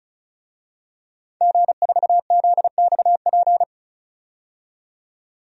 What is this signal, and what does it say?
Morse code, heard as a keyed tone, about 36 wpm, 700 Hz. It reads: G4ZXP
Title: DXCC-100 Call Signs - 5 Characters - 35wpm